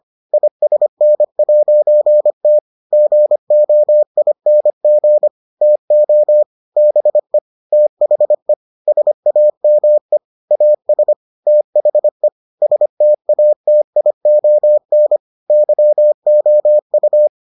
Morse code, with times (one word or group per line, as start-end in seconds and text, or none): 0.3-2.6 ISN'T
2.9-5.3 GOING
5.6-6.4 TO
6.8-7.4 BE
7.7-8.5 THE
8.9-10.2 SAME
10.5-11.1 AS
11.5-12.3 THE
12.6-15.2 STATION
15.5-17.3 YOU